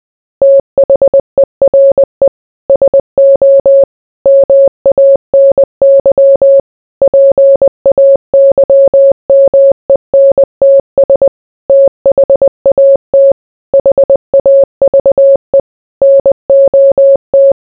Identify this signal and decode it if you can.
Morse code, heard as a keyed tone, about 20 wpm, 550 Hz. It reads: THERE SO MANY PAYMENTS THAT HAVE NOT